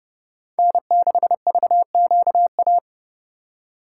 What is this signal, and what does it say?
Morse code, heard as a keyed tone, about 36 wpm, 700 Hz. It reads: N6VQA
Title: Speed Racing - USA 1x3 Call Signs 30wpm